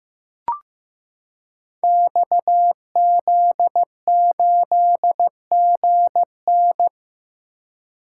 Morse code, heard as a keyed tone, about 15 wpm, 700 Hz. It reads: XZ8GN
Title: Code Groups - 5 Letter:Numbers - 15wpm